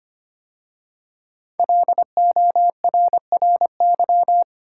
Morse code, heard as a keyed tone, about 25 wpm, 700 Hz. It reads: LORRY
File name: 5 Letter Words - First 3 Letters Given - 25wpm